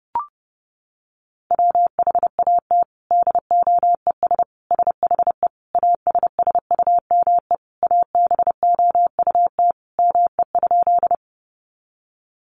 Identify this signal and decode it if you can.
Morse code, heard as a keyed tone, about 30 wpm, 700 Hz. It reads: WHAT DOES SHE ASSUME ABOUT ME?
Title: Sentences from Top 1500 Words 30wpm